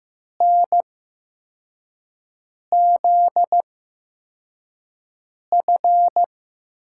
Morse code, heard as a keyed tone, about 15 wpm, spaced as for 4 wpm, 700 Hz. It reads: NZF
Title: USA Call Signs - 8x Character Spacing - 15wpm